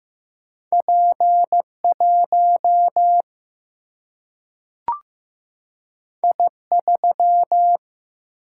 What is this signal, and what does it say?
Morse code, heard as a keyed tone, about 15 wpm, 700 Hz. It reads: P1 I3
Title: Single-Letter Single-Number 15wpm